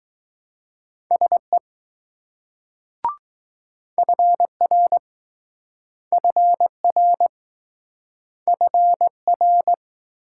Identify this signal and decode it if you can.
Morse code, heard as a keyed tone, about 20 wpm, 700 Hz. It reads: SE FR FR FR